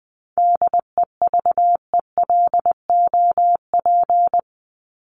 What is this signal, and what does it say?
Morse code, heard as a keyed tone, about 20 wpm, 700 Hz. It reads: DEVELOP
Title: ED Suffix - Top 100 Words with Base Words 20wpm